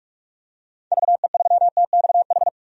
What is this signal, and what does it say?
Morse code, heard as a keyed tone, about 44 wpm, 700 Hz. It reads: VE3TXH
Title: DXCC-100 Call Signs - 6 Characters - 45wpm